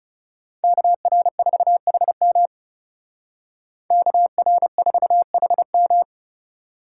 Morse code, heard as a keyed tone, about 36 wpm, 700 Hz. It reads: KR4HM KR4HM
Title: Speed Racing - USA 2x2 Call Signs 30wpm